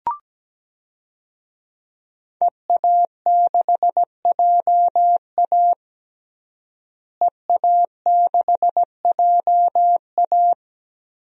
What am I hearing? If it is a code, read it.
Morse code, heard as a keyed tone, about 17 wpm, 700 Hz. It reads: EA6JA EA6JA
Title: DXCC-100 Call Signs - 5 Characters - Repeated 2x - 17wpm